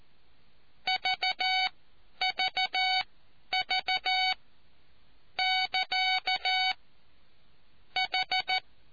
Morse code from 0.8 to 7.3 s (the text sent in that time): VVV<KA>